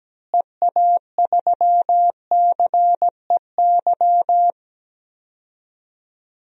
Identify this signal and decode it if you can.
Morse code, heard as a keyed tone, about 17 wpm, 700 Hz. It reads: EA3CEY
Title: DXCC-100 Call Signs - 6 Characters - Repeated 2x - 17wpm